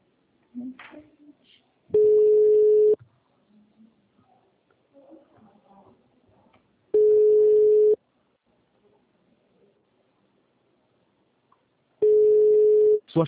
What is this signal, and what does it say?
Ringback tone (Europe, 425 Hz).